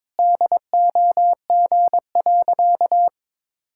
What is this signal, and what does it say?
Morse code, heard as a keyed tone, about 22 wpm, 700 Hz. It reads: DOG.
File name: practice / 22wpm / Sentences with 2-Word Phrases - 22wpm